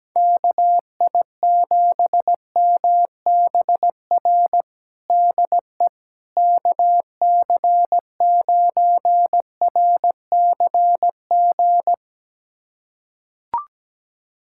Morse code, heard as a keyed tone, about 17 wpm, 700 Hz. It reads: KI7MBR DE KC9RCG
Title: USA Callsign DE Callsign - 6 Characters - 17wpm